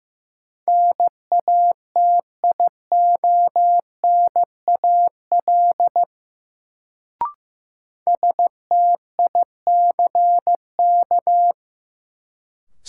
Morse code, heard as a keyed tone, about 16 wpm, 700 Hz. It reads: NATIONAL STICK
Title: Sentences from Top 1200 Words - New Word Drill Alt - 15wpm